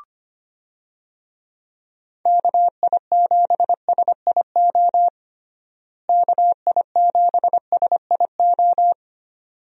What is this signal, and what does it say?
Morse code, heard as a keyed tone, about 25 wpm, 700 Hz. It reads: KI7SIO KI7SIO
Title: USA Call Signs - 6 Characters - Repeated 2x - 25wpm